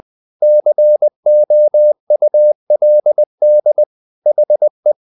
Morse code, heard as a keyed tone, about 20 wpm, 600 Hz. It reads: COULD HE
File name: the old man and the sea part 2 20wpm